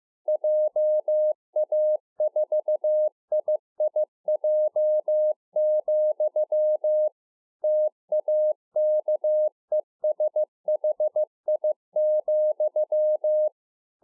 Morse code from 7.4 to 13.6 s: TAKESHI,